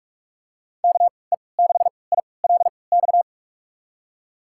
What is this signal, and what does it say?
Morse code, heard as a keyed tone, about 42 wpm, spaced as for 25 wpm, 690 Hz. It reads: KE6ILX
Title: USA Call Signs - 6 Characters - 3x Character Spacing - 45wpm